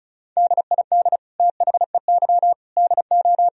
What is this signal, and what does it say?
Morse code, heard as a keyed tone, about 36 wpm, 700 Hz. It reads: DID THEY DO
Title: Sentences from Top 1000 Words - Review Part 4 - 35wpm